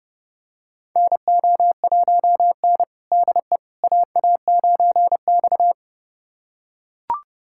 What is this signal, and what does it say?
Morse code, heard as a keyed tone, about 28 wpm, 700 Hz. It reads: NO1N DE AA9X
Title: USA Callsign DE Callsign - 4 Characters - 30wpm